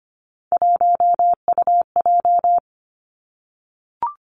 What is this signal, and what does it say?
Morse code, heard as a keyed tone, about 25 wpm, 700 Hz. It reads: 1UJ
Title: Three Challenging Letter-Number Combinations 25wpm